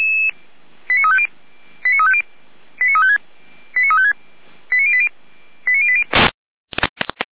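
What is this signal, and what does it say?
Five-tone selective call (ZVEI), heard as quick runs of five tones, tones of about 70 ms. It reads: 89350 89350 89356 89356 89080 89080
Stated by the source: ZVEI five-tone selective call received by SDR.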